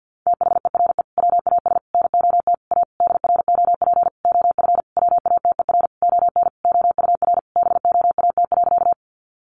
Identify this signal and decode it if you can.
Morse code, heard as a keyed tone, about 50 wpm, 700 Hz. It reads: THERE WAS NOT A DROP OF WATER ON OUR BOAT.
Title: Sentences from Top 1400 Words 50wpm